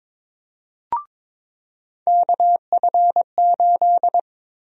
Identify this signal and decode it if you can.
Morse code, heard as a keyed tone, about 22 wpm, 700 Hz. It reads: KF8
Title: USA Call Sign Prefixes - 22wpm